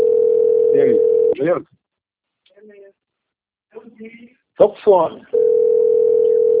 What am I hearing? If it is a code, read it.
Ringback tone (North America), 440 + 480 Hz, with speech.
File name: out- - - - - .43373